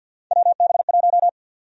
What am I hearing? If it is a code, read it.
Morse code, heard as a keyed tone, about 51 wpm, 700 Hz. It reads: WB1
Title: USA Call Sign Prefixes - 50wpm